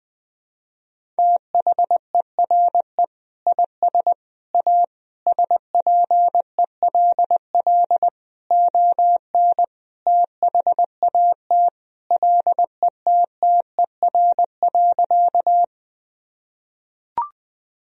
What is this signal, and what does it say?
Morse code, heard as a keyed tone, about 20 wpm, 700 Hz. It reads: THERE IS A SPELL ON THAT LETTER.
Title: Sentences from Top 1000 Words - Review Part 1 - 20wpm